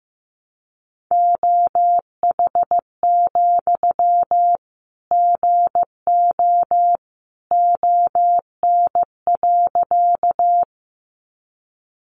Morse code, heard as a keyed tone, about 15 wpm, 700 Hz. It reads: OH, GO ON.
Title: Sentences from Top 500 Words 15wpm